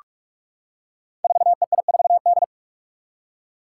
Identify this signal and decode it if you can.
Morse code, heard as a keyed tone, about 44 wpm, 700 Hz. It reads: 4EI4D
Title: Five Challenging Letter-Number Combinations 45wpm